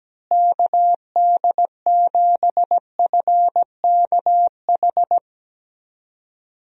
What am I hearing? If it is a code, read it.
Morse code, heard as a keyed tone, about 17 wpm, 700 Hz. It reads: KD7FKH
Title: USA Call Signs - 6 Characters - Repeated 2x - 17wpm